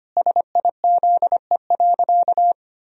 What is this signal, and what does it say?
Morse code, heard as a keyed tone, about 25 wpm, 700 Hz. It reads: SIZE.
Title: Sentences from Top 1000 Words - Review Part 2 - 25wpm